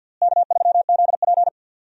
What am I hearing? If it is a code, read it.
Morse code, heard as a keyed tone, about 53 wpm, 700 Hz. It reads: K3ZP
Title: USA Call Signs - 4 Characters - Repeated 3x - 50wpm